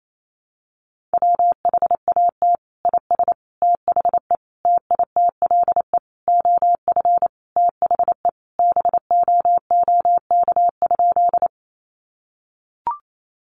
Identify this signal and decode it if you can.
Morse code, heard as a keyed tone, about 28 wpm, 700 Hz. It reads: WHAT IS THE TITLE OF THE BOOK?
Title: Sentences from Top 1200 Words 28wpm